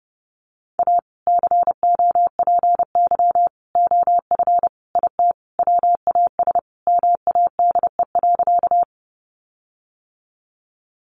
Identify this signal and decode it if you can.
Morse code, heard as a keyed tone, about 30 wpm, 700 Hz. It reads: A COPY OF IT WAS MADE.
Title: Sentences from Top 1400 Words 30wpm